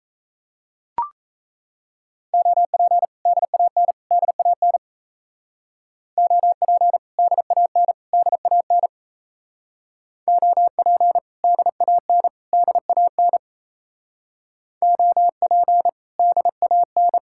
Morse code, heard as a keyed tone, about 35 wpm, 700 Hz. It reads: OP DAN DAN OP DAN DAN OP DAN DAN OP DAN
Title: Speed-Racing Common QSO Phrases - 28wpm